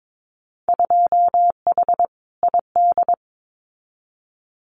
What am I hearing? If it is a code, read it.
Morse code, heard as a keyed tone, about 22 wpm, 700 Hz. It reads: 2H ID
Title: Winter Field Day Exchanges 22wpm